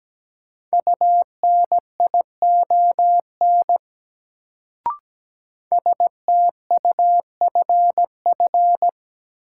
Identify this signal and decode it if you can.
Morse code, heard as a keyed tone, about 17 wpm, 700 Hz. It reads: UNION STUFF
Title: Sentences from Top 1400 Words - New Word Drill Alt - 17wpm